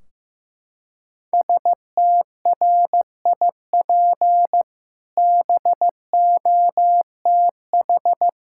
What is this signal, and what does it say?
Morse code, heard as a keyed tone, about 15 wpm, 700 Hz. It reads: STRIP BOTH
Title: Sentences from Top 1600 Words 15wpm